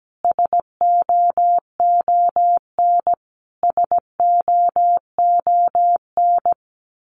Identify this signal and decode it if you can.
Morse code, heard as a keyed tone, about 17 wpm, 700 Hz. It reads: SOON SOON
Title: Top 500 Words Repeated 17wpm